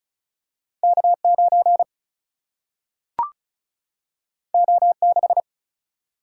Morse code, heard as a keyed tone, about 36 wpm, 700 Hz. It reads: K 9 O 6
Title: Single-Letter Single-Number 35wpm